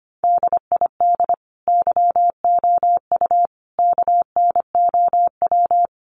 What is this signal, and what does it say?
Morse code, heard as a keyed tone, about 25 wpm, 700 Hz. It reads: DID YOU KNOW
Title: Sentences from Top 100 Words 25wpm